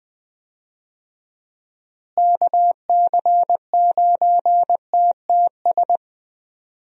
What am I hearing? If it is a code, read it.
Morse code, heard as a keyed tone, about 20 wpm, 700 Hz. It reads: KC9TTS
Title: USA Call Signs - 6 Characters - Repeated 3x - 20wpm